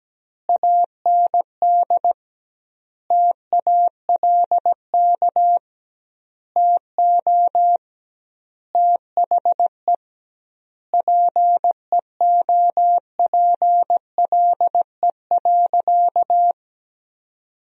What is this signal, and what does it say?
Morse code, heard as a keyed tone, about 17 wpm, 700 Hz. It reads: AND TALK TO THE PEOPLE.
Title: 2x Word-Spacing - Sentences from Top 400 Words - 17wpm